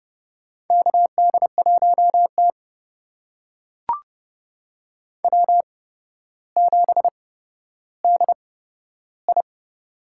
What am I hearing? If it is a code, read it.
Morse code, heard as a keyed tone, about 30 wpm, spaced as for 13 wpm, 700 Hz. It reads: KD1T W7DI
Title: USA Call Signs - 4 Characters - 8x Character Spacing - 30wpm